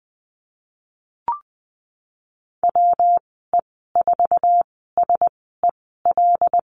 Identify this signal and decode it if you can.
Morse code, heard as a keyed tone, about 19 wpm, spaced as for 14 wpm, 700 Hz. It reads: WE4SEL
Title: 122 Lesson 20g - Number 3 - Call Signs 20wpm